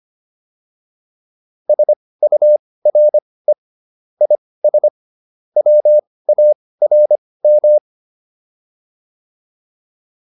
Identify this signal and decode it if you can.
Morse code, heard as a keyed tone, about 25 wpm, spaced as for 18 wpm, 600 Hz. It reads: SURE IS WARM